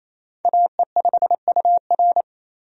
Morse code, heard as a keyed tone, about 28 wpm, 700 Hz. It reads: AE5UR
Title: USA Call Signs - 5 Characters - 28wpm